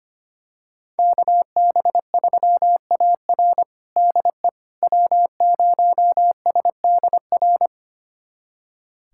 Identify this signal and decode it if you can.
Morse code, heard as a keyed tone, about 25 wpm, 700 Hz. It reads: KB3AR DE W0SDR
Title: USA Callsign DE Callsign - 5 Characters - 25wpm